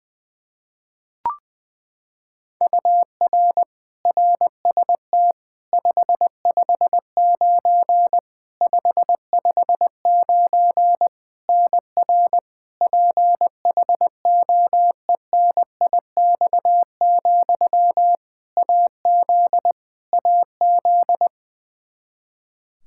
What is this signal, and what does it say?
Morse code, heard as a keyed tone, about 20 wpm, 700 Hz. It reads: UR RST 559 559 NR PHOENIX, AZ AZ
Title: Practice QSO - Exchange 1 - Beginning - 20wpm